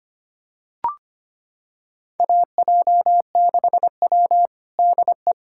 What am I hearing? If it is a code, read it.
Morse code, heard as a keyed tone, about 25 wpm, 700 Hz. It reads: AJ6W DE
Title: USA Callsign DE Callsign - 4 Characters - 25wpm